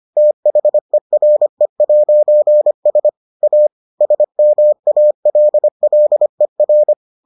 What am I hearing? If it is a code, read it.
Morse code, heard as a keyed tone, about 25 wpm, 600 Hz. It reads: THERE'S A SMALLER